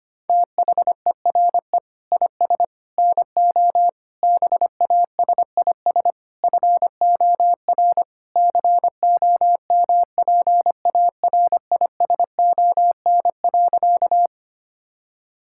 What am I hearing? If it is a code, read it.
Morse code, heard as a keyed tone, about 25 wpm, 700 Hz. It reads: THERE IS NO BASIS FOR COMPARISON.
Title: Sentences from Top 1600 Words 25wpm